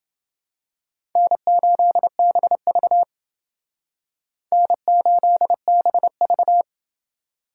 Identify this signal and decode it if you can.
Morse code, heard as a keyed tone, about 29 wpm, 700 Hz. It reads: N8BV N8BV